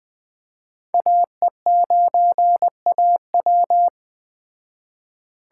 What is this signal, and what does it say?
Morse code, heard as a keyed tone, about 20 wpm, 700 Hz. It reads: AE9AW
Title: Speed Racing - USA 2x2 Call Signs 17wpm